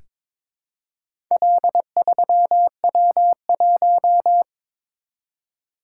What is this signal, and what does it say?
Morse code, heard as a keyed tone, about 22 wpm, 700 Hz. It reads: L3W1
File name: practice / 22wpm / Four Letter-Number Combinations - 22wpm